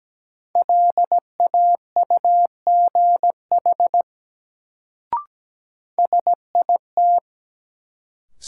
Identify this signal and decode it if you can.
Morse code, heard as a keyed tone, about 17 wpm, 700 Hz. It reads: LAUGH SIT
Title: Sentences from Top 500 Words - New Word Drill - 17wpm